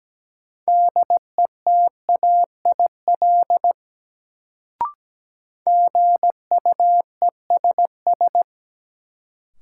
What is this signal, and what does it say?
Morse code, heard as a keyed tone, about 17 wpm, 700 Hz. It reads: DETAIL GUESS
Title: Sentences from Top 1500 Words - New Word Drill - 17wpm